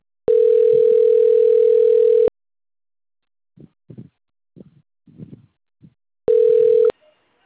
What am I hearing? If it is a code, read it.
Ringback tone (North America), 440 + 480 Hz.
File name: rg-1600-unknown- - - .34199